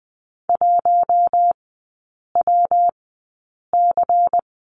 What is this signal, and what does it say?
Morse code, heard as a keyed tone, about 20 wpm, spaced as for 8 wpm, 700 Hz. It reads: 1WC
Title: 121 Lesson 20e - Number 3 - Three Characters 20wpm